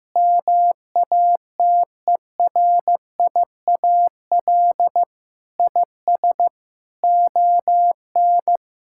Morse code, heard as a keyed tone, about 16 wpm, 700 Hz. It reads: MATERIAL IS ON
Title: Sentences from Top 1000 Words - Review Part 5 - 15wpm